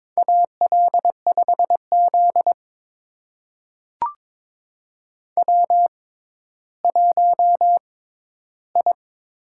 Morse code, heard as a keyed tone, about 22 wpm, spaced as for 12 wpm, 700 Hz. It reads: AL5Z W1I
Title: USA Call Signs - 4 Characters - 6x Character Spacing - 22wpm